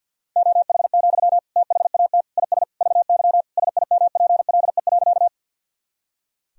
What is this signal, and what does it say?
Morse code, heard as a keyed tone, about 53 wpm, 700 Hz. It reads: OH, THAT IS VY SIMPLE.
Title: Sentences from Top 500 Words and Abbreviations 50wpm